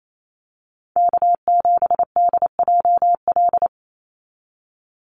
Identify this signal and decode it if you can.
Morse code, heard as a keyed tone, about 28 wpm, 700 Hz. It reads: K7DJL
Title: NA SOTA Chaser Callsigns 28wpm